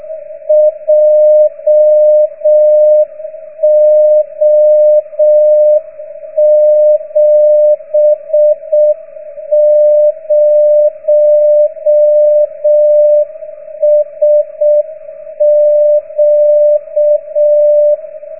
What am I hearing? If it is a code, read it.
Morse code, heard as a keyed tone, about 6 wpm, 610 Hz. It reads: JO70SQ